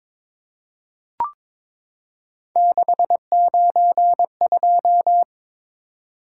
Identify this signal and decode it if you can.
Morse code, heard as a keyed tone, about 23 wpm, 700 Hz. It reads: 692
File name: practice / 22wpm / Three Letter-Number Combinations - 22wpm